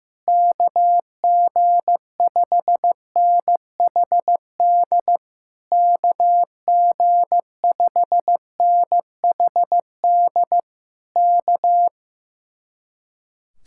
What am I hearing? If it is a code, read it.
Morse code, heard as a keyed tone, about 15 wpm, 700 Hz. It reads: KG5NHD KG5NHD K
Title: Speed-Racing Common QSO Phrases Encore - 15wpm